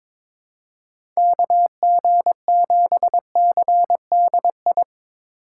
Morse code, heard as a keyed tone, about 22 wpm, 700 Hz. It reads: KG7CDI
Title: USA Call Signs - 6 Characters - 6x Character Spacing - 22wpm